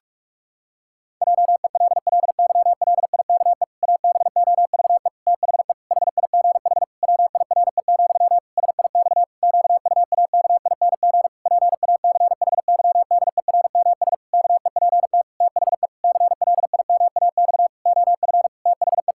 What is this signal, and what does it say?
Morse code, heard as a keyed tone, about 48 wpm, 700 Hz. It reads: JELLYLIKE ABOVE THE HIGH WIRE, SIX QUAKING PACHYDERMS KEPT THE CLIMAX OF THE
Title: Pangram Challenge v2 - 45wpm